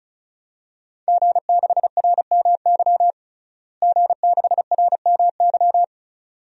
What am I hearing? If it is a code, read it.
Morse code, heard as a keyed tone, about 36 wpm, 700 Hz. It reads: G6RMY G6RMY
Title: DXCC-100 Call Signs - 5 Characters - Repeated 2x - 35wpm